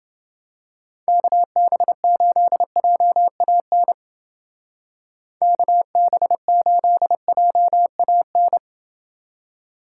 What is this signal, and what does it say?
Morse code, heard as a keyed tone, about 29 wpm, 700 Hz. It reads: KB8JAN KB8JAN